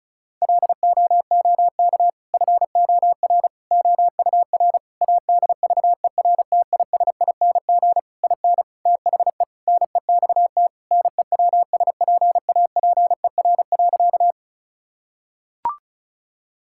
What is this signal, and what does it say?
Morse code, heard as a keyed tone, about 36 wpm, 700 Hz. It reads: LOOK FOR OUR ADVERTISING IN THE NEXT NEWSPAPER.
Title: Sentences from Top 1000 Words - Review Part 6 - 35wpm